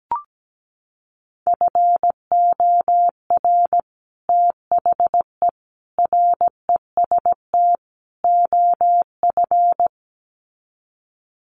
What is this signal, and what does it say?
Morse code, heard as a keyed tone, about 17 wpm, 700 Hz. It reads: FOR THE REST OF
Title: Sets of 4 Words 17wpm